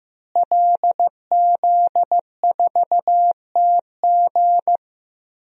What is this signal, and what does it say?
Morse code, heard as a keyed tone, about 15 wpm, 700 Hz. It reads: LZ4TG
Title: DXCC-100 Call Signs - 5 Characters - Repeated 2x - 15wpm